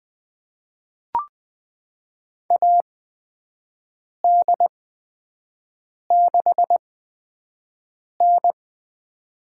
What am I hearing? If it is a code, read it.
Morse code, heard as a keyed tone, about 21 wpm, spaced as for 5 wpm, 700 Hz. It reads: AD6N
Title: USA Call Signs - 4 Characters - 8x Character Spacing - 20wpm